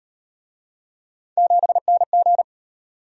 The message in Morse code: 7NG

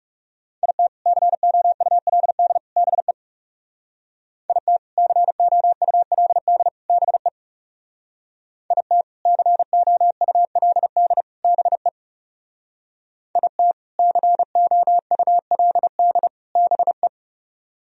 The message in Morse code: IT COULD BE IT COULD BE IT COULD BE IT COULD BE

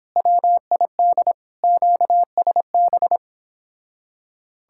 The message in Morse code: WID QSB

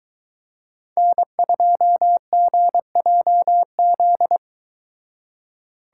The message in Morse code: N2GJZ

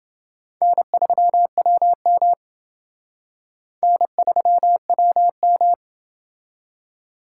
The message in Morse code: N3WM N3WM

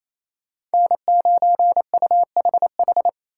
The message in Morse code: N9UHH

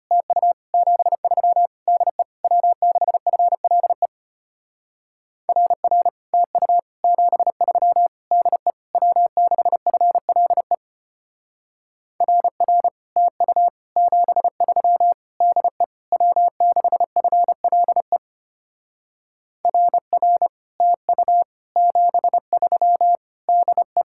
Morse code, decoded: TU 73 DE W6FLE RR TU 73 DE W6FLE RR TU 73 DE W6FLE RR TU 73 DE

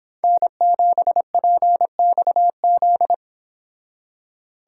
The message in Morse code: N7PXZ

